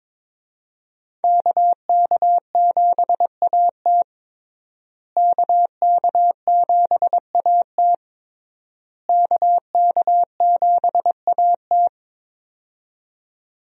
Morse code, heard as a keyed tone, about 21 wpm, 700 Hz. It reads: KK7AT KK7AT KK7AT
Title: USA Call Signs - 5 Characters - Repeated 3x - 22wpm